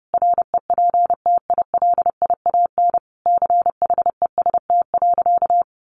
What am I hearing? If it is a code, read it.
Morse code, heard as a keyed tone, about 30 wpm, 700 Hz. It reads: REPTILIAN CHEST.
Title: A Sound Of Thunder 30wpm